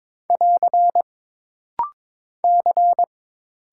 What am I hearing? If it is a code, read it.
Morse code, heard as a keyed tone, about 22 wpm, 700 Hz. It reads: <AR> C
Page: This rapid-fire timing is geared to push your ICR (Instant Character Recognition) skills.